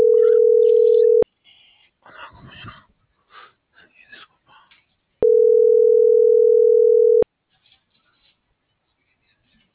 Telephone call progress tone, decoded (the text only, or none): ringback tone